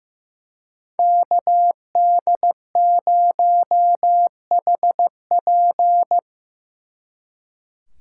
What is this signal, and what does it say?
Morse code, heard as a keyed tone, about 18 wpm, 700 Hz. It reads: KD0HP
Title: Speed Racing - USA Call Signs 15wpm